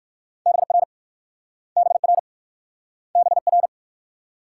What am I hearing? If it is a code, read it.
Morse code, heard as a keyed tone, about 54 wpm, 690 Hz. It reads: BR BR BR
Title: Speed-Racing Multi-letter Phonemes - 45wpm